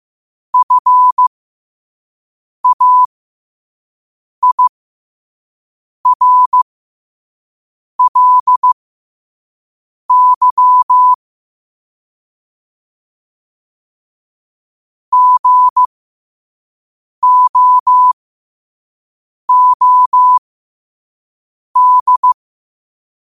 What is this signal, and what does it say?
Morse code, heard as a keyed tone, about 15 wpm, spaced as for 5 wpm, 1000 Hz. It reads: FAIRLY GOOD
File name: qotdfiles / 5wpm